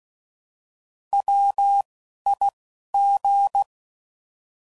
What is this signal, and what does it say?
Morse code, heard as a keyed tone, about 16 wpm, spaced as for 11 wpm, 800 Hz. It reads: WIG
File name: Prefixes / By First Letter / Ws